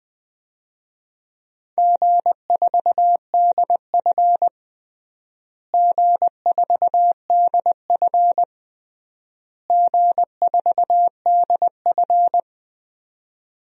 Morse code, heard as a keyed tone, about 19 wpm, 690 Hz. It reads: G4DF G4DF G4DF
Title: DXCC-100 Call Signs - 4 Characters - Repeated 3x - 20wpm